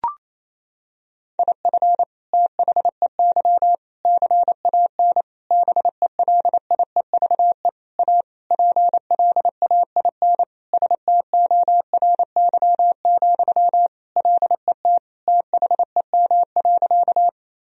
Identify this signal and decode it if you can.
Morse code, heard as a keyed tone, about 28 wpm, 700 Hz. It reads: IF THEY CAN BELIEVE A PLAIN STORY, LET THEM.
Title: Sentences from Top 1000 Words - Review Part 2 - 28wpm